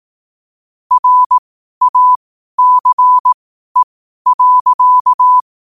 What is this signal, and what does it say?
Morse code, heard as a keyed tone, about 18 wpm, spaced as for 12 wpm, 1000 Hz. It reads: RACE.
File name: qotdfiles / 12wpm